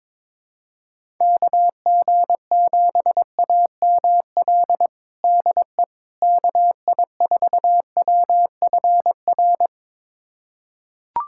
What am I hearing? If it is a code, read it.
Morse code, heard as a keyed tone, about 22 wpm, 700 Hz. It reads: KG7AML DE KI4WFR
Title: USA Callsign DE Callsign 22wpm